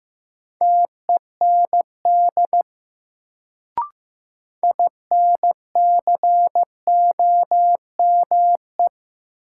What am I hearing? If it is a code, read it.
Morse code, heard as a keyed tone, about 15 wpm, 700 Hz. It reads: TEND INCOME
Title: Sentences from Top 1000 Words - Review Part 5 - New Word Drill - 15wpm